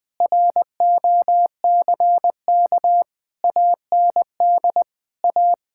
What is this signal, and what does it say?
Morse code, heard as a keyed tone, about 21 wpm, 700 Hz. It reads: ROCK AND A
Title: Sentences with 8-Word Phrases - Spoken Beginning - 20wpm